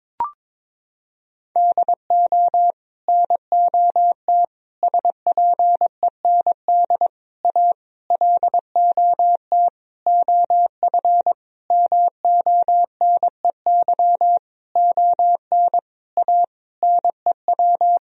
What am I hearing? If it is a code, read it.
Morse code, heard as a keyed tone, about 22 wpm, 700 Hz. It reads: DO NOT SPEND A LOT OF MONEY ON A NEW
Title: Sentences from Top 1000 Words - Review Part 4 - 22wpm